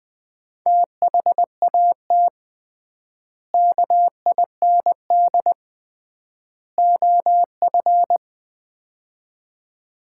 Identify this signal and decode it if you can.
Morse code, heard as a keyed tone, about 20 wpm, 700 Hz. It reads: THAT KIND OF
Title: 3x Word-Spacing - Sets of 3 Words - 20wpm